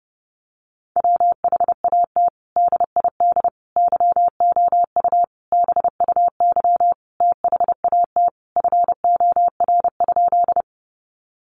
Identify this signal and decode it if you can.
Morse code, heard as a keyed tone, about 30 wpm, 700 Hz. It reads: WHAT DID YOU BUY THAT FOR?
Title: Sentences from Top 700 Words 30wpm